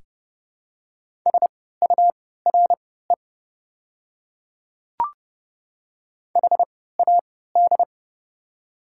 Morse code, heard as a keyed tone, about 32 wpm, spaced as for 17 wpm, 700 Hz. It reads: SURE HAD
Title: 3x Character Spacing - Top 500 Words - 30wpm